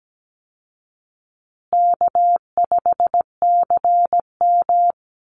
Morse code, heard as a keyed tone, about 17 wpm, 700 Hz. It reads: K5CM
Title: Call Signs Repeated 2x - 17wpm